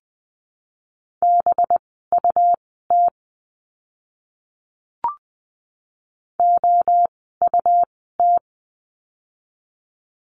Morse code, heard as a keyed tone, about 20 wpm, spaced as for 14 wpm, 700 Hz. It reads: BUT OUT
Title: 237 Lesson 38f - Letter q - Words 20wpm